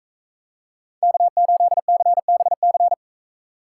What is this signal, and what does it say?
Morse code, heard as a keyed tone, about 44 wpm, 700 Hz. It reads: K8CBC